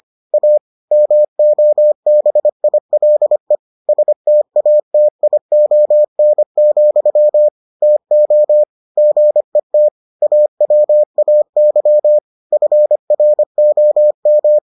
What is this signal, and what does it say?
Morse code, heard as a keyed tone, about 25 wpm, 600 Hz. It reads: A MOBILE STATION, TO GET AWAY FROM